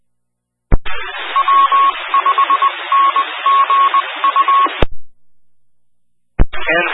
Morse code, heard as a keyed tone, about 20 wpm, 1100 Hz, with noise. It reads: W5NGU
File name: W5NGU Denton County ARC and Skywarn 146920 Mhz Repeater